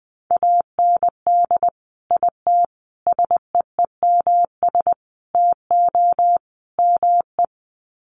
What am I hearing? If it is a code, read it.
Morse code, heard as a keyed tone, about 20 wpm, 700 Hz. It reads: AND IT SEEMS TO ME
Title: Sets of 5 Words Encore2 20wpm